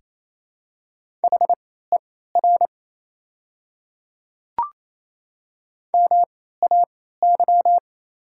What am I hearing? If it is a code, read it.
Morse code, heard as a keyed tone, about 28 wpm, spaced as for 16 wpm, 690 Hz. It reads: HER MAY
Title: 3x Character Spacing - Top 500 Words - 28wpm